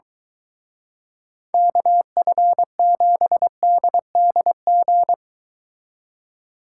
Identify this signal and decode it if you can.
Morse code, heard as a keyed tone, about 23 wpm, 700 Hz. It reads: KF7DDG